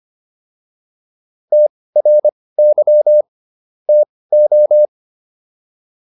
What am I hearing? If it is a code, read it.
Morse code, heard as a keyed tone, about 25 wpm, spaced as for 18 wpm, 600 Hz. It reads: TRY TO
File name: WD104 18